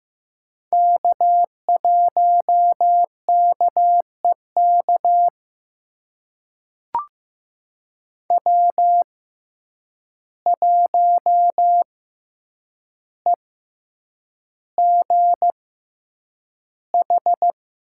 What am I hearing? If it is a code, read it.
Morse code, heard as a keyed tone, about 15 wpm, spaced as for 8 wpm, 700 Hz. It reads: K1KEK W1EGH
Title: USA Call Signs - 5 Characters - 6x Character Spacing - 15wpm